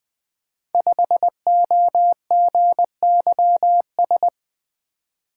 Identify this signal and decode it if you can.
Morse code, heard as a keyed tone, about 20 wpm, 700 Hz. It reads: 5OGYS
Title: Code Groups - 5 Letter:Numbers - 20wpm